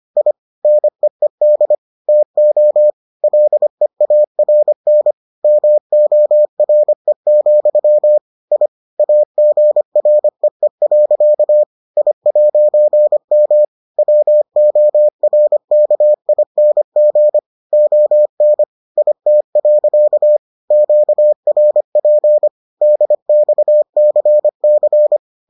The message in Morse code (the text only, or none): I NEED TO LEARN MORE, I AGREE. I'M WORKING ON IT. QRP DXCC